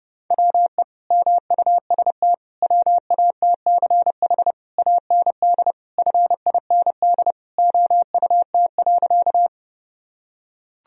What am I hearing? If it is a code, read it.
Morse code, heard as a keyed tone, about 30 wpm, 700 Hz. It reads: WE MUST WATCH AND FIND OUT.
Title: Sentences from Top 1000 Words - Review Part 1 - 30wpm